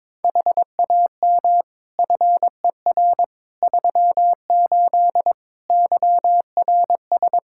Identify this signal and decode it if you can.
Morse code, heard as a keyed tone, about 22 wpm, 700 Hz. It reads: HAM FER 38 YRS
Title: K1USN SST QSO Practice 22wpm